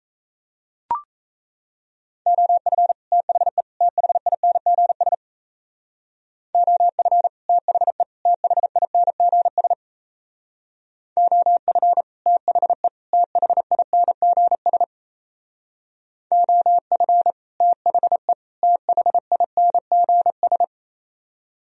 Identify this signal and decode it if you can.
Morse code, heard as a keyed tone, about 34 wpm, 700 Hz. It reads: OF THE THINGS OF THE THINGS OF THE THINGS OF THE THINGS
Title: Speed-Racing - Sets of 3 Words 28wpm